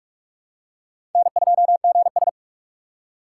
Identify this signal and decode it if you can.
Morse code, heard as a keyed tone, about 46 wpm, 700 Hz. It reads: N2GS
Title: USA Call Signs - 2x Character Spacing - 45wpm